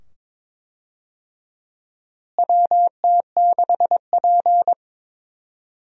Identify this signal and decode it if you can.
Morse code, heard as a keyed tone, about 22 wpm, 700 Hz. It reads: WT6P
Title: USA Call Signs - 4 Characters - Repeated 2x - 22wpm